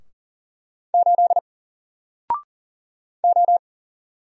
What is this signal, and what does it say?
Morse code, heard as a keyed tone, about 40 wpm, 700 Hz. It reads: 8 O
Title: Single Character (Letter:Number:Punctuation:Prosign) 40wpm